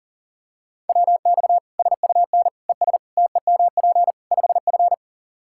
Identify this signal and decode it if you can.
Morse code, heard as a keyed tone, about 40 wpm, 700 Hz. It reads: WX SUN ES TEMP 5F
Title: Practice QSO - Exchange 2 - Info - 40wpm